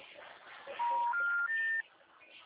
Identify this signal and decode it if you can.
Special information tone.
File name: out- -2009- - - .43984